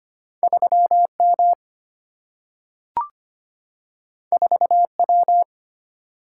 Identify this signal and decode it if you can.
Morse code, heard as a keyed tone, about 25 wpm, 700 Hz. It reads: 3M 4W
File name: Two Letter-Number Combinations - 25wpm